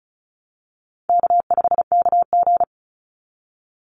Morse code, heard as a keyed tone, about 34 wpm, 700 Hz. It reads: K5KG
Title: Easy Call Signs Repeated 2x - 35wpm